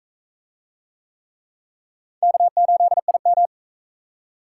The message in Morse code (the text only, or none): K8IM